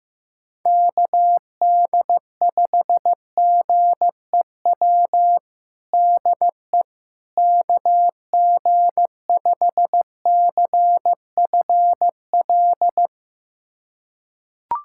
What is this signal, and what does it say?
Morse code, heard as a keyed tone, about 15 wpm, 700 Hz. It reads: KD5GEW DE KG5CFL
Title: USA Callsign DE Callsign - 6 Characters - 15wpm